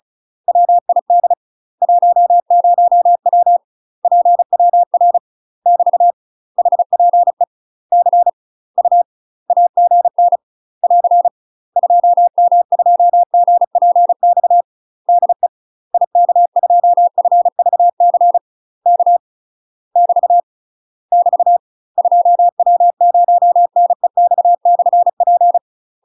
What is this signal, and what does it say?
Morse code, heard as a keyed tone, about 35 wpm, 700 Hz. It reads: WID 10W PWR <BT> HPE C U AGN <AR> 2M2GPX DE IK2FVC K <BT> <BT> 2W0NEX/P
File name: UK MASTER QSOs TRAINING / 35WPM / UK MASTER QSOs 0009